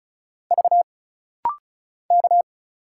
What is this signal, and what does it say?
Morse code, heard as a keyed tone, about 36 wpm, 700 Hz. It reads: V K
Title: Single Letter-Number - Rapid-Fire 35wpm